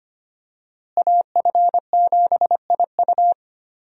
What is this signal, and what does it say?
Morse code, heard as a keyed tone, about 25 wpm, 700 Hz. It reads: AF7IU
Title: USA Call Signs - 5 Characters - Repeated 2x - 25wpm